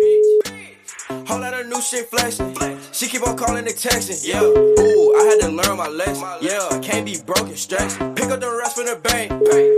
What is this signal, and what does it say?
Ringback tone (Europe, 425 Hz), with speech.